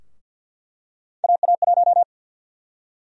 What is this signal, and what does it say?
Morse code, heard as a keyed tone, about 48 wpm, 700 Hz. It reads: AA1
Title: USA Call Sign Prefixes - 50wpm